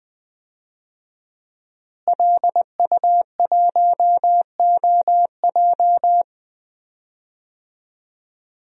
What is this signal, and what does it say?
Morse code, heard as a keyed tone, about 20 wpm, 700 Hz. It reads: LU1OJ
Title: DXCC-100 Call Signs - 5 Characters - Repeated 2x - 20wpm